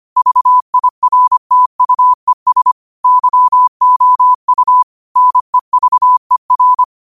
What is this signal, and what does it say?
Morse code, heard as a keyed tone, about 25 wpm, 1000 Hz. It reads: VIRTUES YOU NEVER